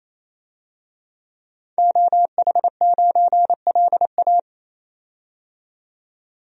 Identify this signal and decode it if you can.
Morse code, heard as a keyed tone, about 28 wpm, 700 Hz. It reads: OH9LA
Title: DXCC-100 Call Signs - 5 Characters - 28wpm